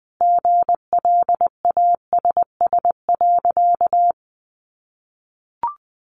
Morse code, heard as a keyed tone, about 20 wpm, 700 Hz. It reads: GLASS.
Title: Sentences from Top 1300 Words 20wpm